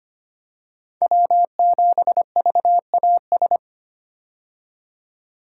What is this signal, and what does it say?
Morse code, heard as a keyed tone, about 25 wpm, 700 Hz. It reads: W7VAS
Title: K1USN SST QSO Practice - 25wpm